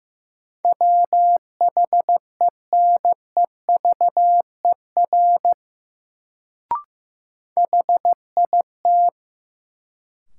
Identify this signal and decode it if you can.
Morse code, heard as a keyed tone, about 15 wpm, 700 Hz. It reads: WHENEVER HIT
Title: Sentences from Top 1200 Words - New Word Drill - 15wpm